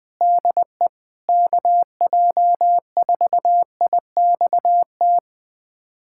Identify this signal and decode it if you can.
Morse code, heard as a keyed tone, about 21 wpm, 700 Hz. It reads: DE KJ4IXT
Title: USA Callsign DE Callsign - 6 Characters - 20wpm